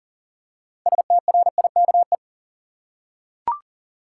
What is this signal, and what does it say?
Morse code, heard as a keyed tone, about 40 wpm, 700 Hz. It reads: STRIKE
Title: Six Letter Words - 40wpm